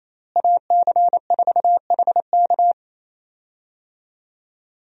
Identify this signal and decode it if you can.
Morse code, heard as a keyed tone, about 27 wpm, 700 Hz. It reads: AC4HK
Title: USA Call Signs - 5 Characters - Repeated 2x - 28wpm